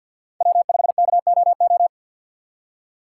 Morse code, heard as a keyed tone, about 47 wpm, 700 Hz. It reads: W5KOO
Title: USA Call Signs - 5 Characters - 2x Character Spacing - 50wpm